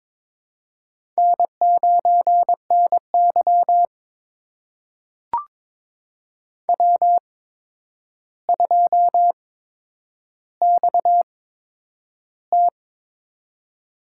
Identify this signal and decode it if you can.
Morse code, heard as a keyed tone, about 23 wpm, spaced as for 10 wpm, 700 Hz. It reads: N9NY W2XT
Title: USA Call Signs - 4 Characters - 8x Character Spacing - 22wpm